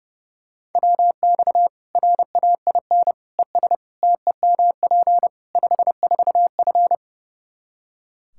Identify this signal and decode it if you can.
Morse code, heard as a keyed tone, about 32 wpm, 700 Hz. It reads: WX RAIN ES TEMP 54F
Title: Practice QSO - Exchange 2 - Info - 30wpm